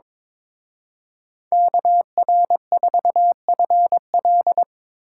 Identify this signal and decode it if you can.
Morse code, heard as a keyed tone, about 25 wpm, 700 Hz. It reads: KR4FL
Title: Speed Racing - USA 2x2 Call Signs 22wpm